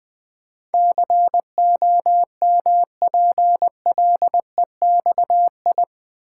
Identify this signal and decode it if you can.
Morse code, heard as a keyed tone, about 20 wpm, 700 Hz. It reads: COMPLEXI
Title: Build A Word - 4 Letters At A Time - 10-Letter Words - 20wpm